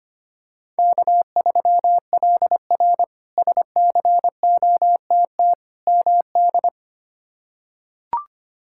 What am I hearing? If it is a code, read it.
Morse code, heard as a keyed tone, about 25 wpm, 700 Hz. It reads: K3LR SCOTT MD
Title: K1USN SST QSO Practice - 25wpm